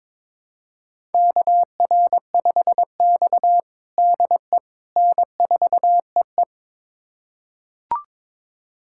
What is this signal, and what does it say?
Morse code, heard as a keyed tone, about 22 wpm, 700 Hz. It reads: KR5X DE N4EE
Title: USA Callsign DE Callsign - 4 Characters - 22wpm